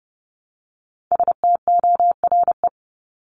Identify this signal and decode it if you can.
Morse code, heard as a keyed tone, about 30 wpm, 700 Hz. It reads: STORE
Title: Top 501-750 Words 30wpm